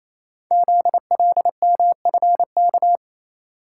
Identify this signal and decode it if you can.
Morse code, heard as a keyed tone, about 29 wpm, 700 Hz. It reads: ZLMFK
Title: Code Groups - 5 Letters - 28wpm